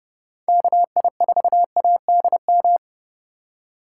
Morse code, heard as a keyed tone, about 30 wpm, 700 Hz. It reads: KI4ADM